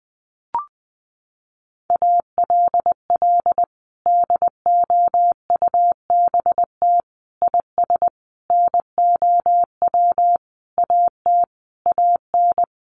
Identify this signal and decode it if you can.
Morse code, heard as a keyed tone, about 20 wpm, 700 Hz. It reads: ALL DOUBT IS NOW AT AN
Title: Sentences from Top 1500 Words 20wpm